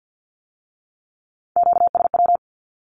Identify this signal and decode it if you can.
Morse code, heard as a keyed tone, about 50 wpm, 700 Hz. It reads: QSW
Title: Q-Codes with Spoken Meaning - 50wpm